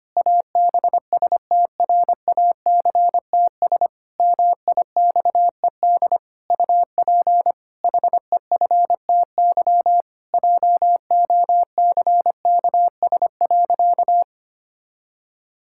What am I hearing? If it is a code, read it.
Morse code, heard as a keyed tone, about 25 wpm, 700 Hz. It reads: ABSTRACTS MIXED UP HEFTY JOCKS.
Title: Pangram Challenge v2 - 25wpm